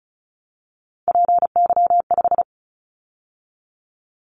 Morse code, heard as a keyed tone, about 36 wpm, 700 Hz. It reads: PY5
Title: Callsign Prefixes for Top 100 DXCC - 35wpm